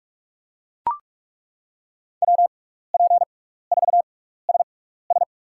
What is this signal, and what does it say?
Morse code, heard as a keyed tone, about 42 wpm, spaced as for 15 wpm, 700 Hz. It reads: WP4SS
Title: USA Call Signs - 5 Characters - 6x Character Spacing - 45wpm